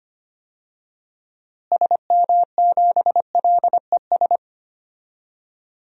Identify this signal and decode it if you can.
Morse code, heard as a keyed tone, about 25 wpm, 700 Hz. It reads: SM7LES
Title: DXCC-100 Call Signs - 6 Characters - Repeated 2x - 25wpm